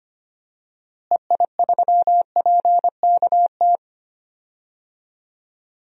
Morse code, heard as a keyed tone, about 25 wpm, 700 Hz. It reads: EI3PKT
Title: DXCC-100 Call Signs - 6 Characters - Repeated 2x - 25wpm